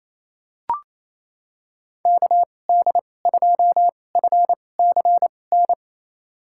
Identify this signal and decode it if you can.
Morse code, heard as a keyed tone, about 27 wpm, spaced as for 20 wpm, 700 Hz. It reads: KD2FCN
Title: USA Call Signs - 6 Characters - 2x Character Spacing - 28wpm